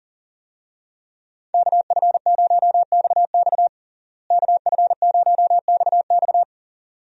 Morse code, heard as a keyed tone, about 40 wpm, 700 Hz. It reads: KF0XX KF0XX
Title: USA Call Signs - 5 Characters - Repeated 2x - 40wpm